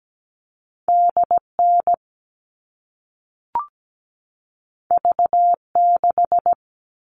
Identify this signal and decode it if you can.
Morse code, heard as a keyed tone, about 17 wpm, 700 Hz. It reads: DN V6
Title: All Cut Numbers - 2-Digit 17wpm